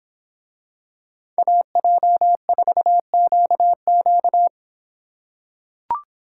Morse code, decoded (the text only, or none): AJ4QQ